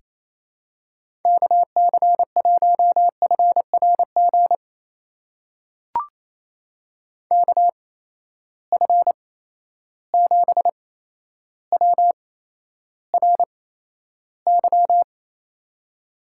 Morse code, heard as a keyed tone, about 28 wpm, spaced as for 12 wpm, 700 Hz. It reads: KC1FRG KF7WRY
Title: USA Call Signs - 6 Characters - 8x Character Spacing - 28wpm